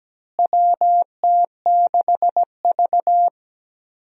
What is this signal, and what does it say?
Morse code, heard as a keyed tone, about 17 wpm, 700 Hz. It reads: WT6V
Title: USA Call Signs - 4 Characters - Repeated 3x - 17wpm